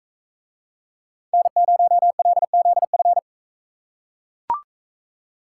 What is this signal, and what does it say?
Morse code, heard as a keyed tone, about 44 wpm, 700 Hz. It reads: N0LZF